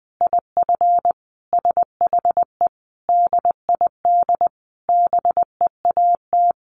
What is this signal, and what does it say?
Morse code, heard as a keyed tone, about 20 wpm, 700 Hz. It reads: IF SHE DID BEAT
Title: Sentences from Top 1300 Words 20wpm